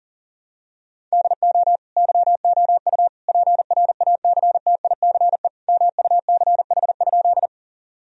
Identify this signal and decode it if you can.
Morse code, heard as a keyed tone, about 40 wpm, 700 Hz. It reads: DO YOU PRACTICE MUCH?
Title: Sentences from Top 1000 Words - Review Part 4 - 40wpm